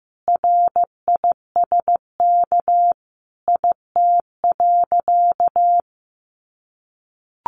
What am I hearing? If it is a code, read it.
Morse code, heard as a keyed tone, about 15 wpm, 700 Hz. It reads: RISK IT.
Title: Sentences from Top 1500 Words 15wpm